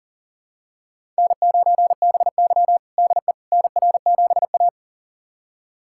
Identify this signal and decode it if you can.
Morse code, heard as a keyed tone, about 40 wpm, 700 Hz. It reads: N9BY DE NR7A
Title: USA Callsign DE Callsign - 4 Characters - 40wpm